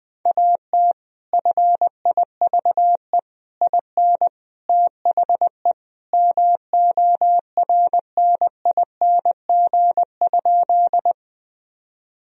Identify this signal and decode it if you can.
Morse code, heard as a keyed tone, about 20 wpm, 700 Hz. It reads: AT FIVE IN THE MORNING?
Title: Sentences from Top 1000 Words - Review Part 2 - 20wpm